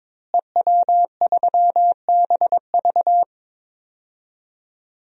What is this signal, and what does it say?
Morse code, heard as a keyed tone, about 22 wpm, 700 Hz. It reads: EW3BV
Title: DXCC-100 Call Signs - 5 Characters - Repeated 2x - 22wpm